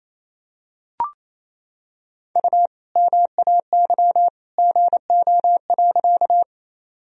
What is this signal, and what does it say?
Morse code, heard as a keyed tone, about 28 wpm, 700 Hz. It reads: U MAY GO.
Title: Sentences from Top 100 Words and Abbreviations 28wpm